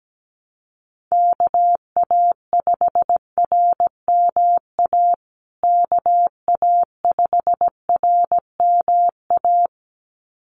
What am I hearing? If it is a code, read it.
Morse code, heard as a keyed tone, about 17 wpm, 700 Hz. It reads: KA5RMA KA5RMA
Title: USA Call Signs Repeated 2x 17wpm